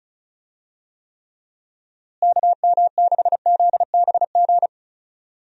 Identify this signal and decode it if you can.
Morse code, heard as a keyed tone, about 35 wpm, 700 Hz. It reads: KM6ZBG
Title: USA Call Signs - 6 Characters - Repeated 2x - 35wpm